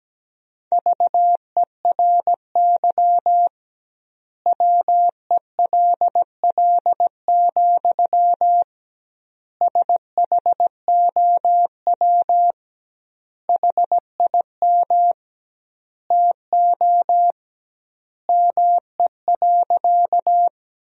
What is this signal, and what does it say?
Morse code, heard as a keyed tone, about 17 wpm, 700 Hz. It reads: VERY WELL, SHOW HIM TO ME.
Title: 2x Word-Spacing - Sentences from Top 200 Words - 17wpm